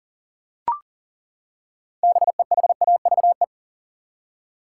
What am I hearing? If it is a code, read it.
Morse code, heard as a keyed tone, about 40 wpm, 700 Hz. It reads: BEHAVE
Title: Six Letter Words - 40wpm